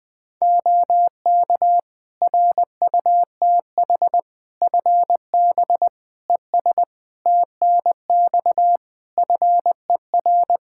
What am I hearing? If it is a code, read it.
Morse code, heard as a keyed tone, about 20 wpm, 700 Hz. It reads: OK RUTH FB ES TNX FER
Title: Practice QSO - Exchange 2 - Info - 20wpm